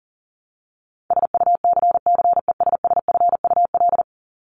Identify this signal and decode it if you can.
Morse code, heard as a keyed tone, about 40 wpm, 700 Hz. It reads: SUCCESSFUL
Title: LY Suffix - Top 100 Words with Base Words 40wpm